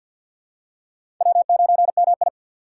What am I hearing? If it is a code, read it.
Morse code, heard as a keyed tone, about 46 wpm, 700 Hz. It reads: W9MI
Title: USA Call Signs - 4 Characters - 2x Character Spacing - 50wpm